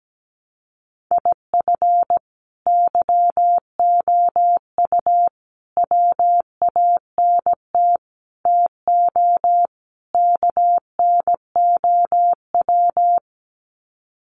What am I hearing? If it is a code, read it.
Morse code, heard as a keyed tone, about 17 wpm, 700 Hz. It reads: IF YOU WANT TO KNOW
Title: Sets of 5 Words Encore2 17wpm